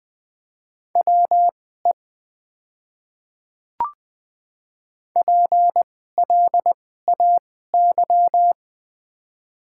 Morse code, heard as a keyed tone, about 20 wpm, spaced as for 14 wpm, 700 Hz. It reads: WE PLAY
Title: 2x Character Spacing - Top 500 Words - 20wpm